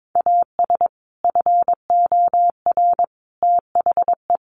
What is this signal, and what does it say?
Morse code, heard as a keyed tone, about 22 wpm, 700 Hz. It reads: AS FOR THE
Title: Sentences from Top 1100 Words 22wpm